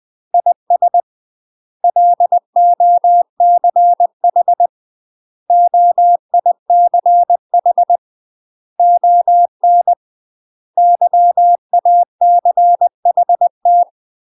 Morse code, heard as a keyed tone, about 20 wpm, 700 Hz. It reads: IS LOCH OICH ON YACHT
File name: UK MASTER QSOs TRAINING / 20WPM / UK MASTER QSOs 0025